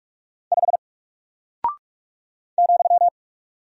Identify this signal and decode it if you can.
Morse code, heard as a keyed tone, about 44 wpm, 700 Hz. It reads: 5 ,
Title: Single Character (Letter:Number:Punctuation:Prosign) Alt 45wpm